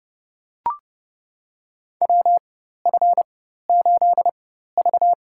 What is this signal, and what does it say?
Morse code, heard as a keyed tone, about 32 wpm, spaced as for 14 wpm, 700 Hz. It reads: WF8V
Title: USA Call Signs - 4 Characters - 4x Character Spacing - 30wpm